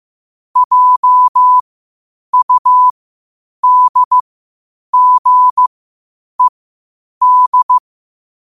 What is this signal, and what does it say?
Morse code, heard as a keyed tone, about 15 wpm, spaced as for 8 wpm, 1000 Hz. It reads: JUDGED
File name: qotdfiles / 8wpm